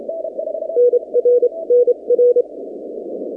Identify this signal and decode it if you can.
Morse code, heard as a keyed tone, about 27 wpm, spaced as for 24 wpm, 490 Hz. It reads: NR NR
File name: 2019-cw-wpx-kt5j-k5pi